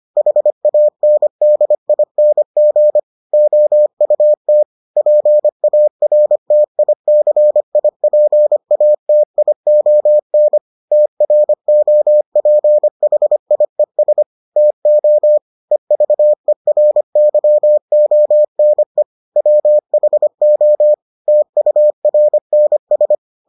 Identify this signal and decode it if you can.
Morse code, heard as a keyed tone, about 25 wpm, 600 Hz. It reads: HANDING OUT PARTICIPATION TROPHIES TO EVERYONE WHO TURNS